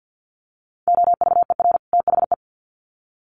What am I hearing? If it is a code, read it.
Morse code, heard as a keyed tone, about 50 wpm, 700 Hz. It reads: OVER THE
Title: Sets of 2 Words 50wpm